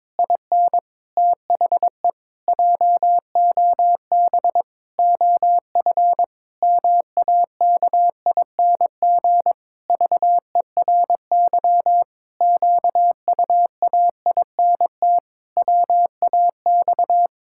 Morse code, heard as a keyed tone, about 21 wpm, 700 Hz. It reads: IN THE JOB OF MAKING VERY QUAINT WAX
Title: Pangram Challenge v2 - 22wpm